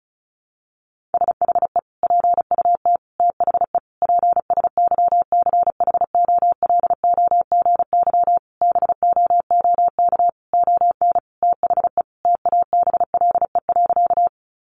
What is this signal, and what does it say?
Morse code, heard as a keyed tone, about 34 wpm, 700 Hz. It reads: SHE PUT THE PSYCHOLOGY BOOK ON THE TABLE.
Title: Sentences from Top 1400 Words 35wpm